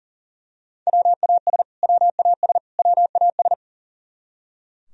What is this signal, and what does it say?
Morse code, heard as a keyed tone, about 40 wpm, 700 Hz. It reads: WAS WAS WAS
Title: Top 100 Words Repeated 40wpm